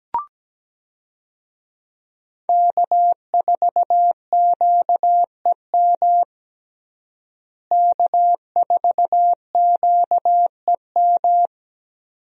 Morse code, heard as a keyed tone, about 17 wpm, 700 Hz. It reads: K4QEM K4QEM
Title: USA Call Signs - 5 Characters - Repeated 2x - 17wpm